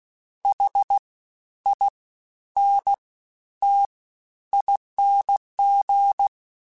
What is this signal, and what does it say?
Morse code, heard as a keyed tone, about 16 wpm, 800 Hz. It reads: H I N T ING
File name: Special INGs-mp3s / Special INGs-mp3s / Fast ing @15